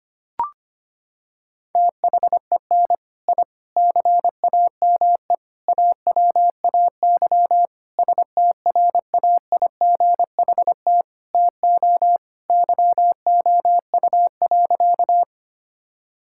Morse code, heard as a keyed tone, about 25 wpm, 700 Hz. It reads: THEN I CAME AWAY STRAIGHT TO YOU.
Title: Sentences from Top 1000 Words - Review Part 6 - 25wpm